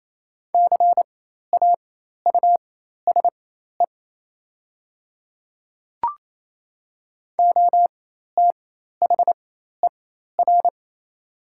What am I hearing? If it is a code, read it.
Morse code, heard as a keyed tone, about 27 wpm, spaced as for 13 wpm, 700 Hz. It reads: CAUSE OTHER
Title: 4x Character Spacing - Top 200 Words - 28wpm